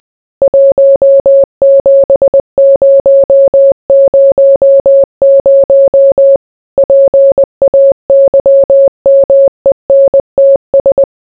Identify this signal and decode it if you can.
Morse code, heard as a keyed tone, about 20 wpm, 550 Hz. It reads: 17000 PAYMENTS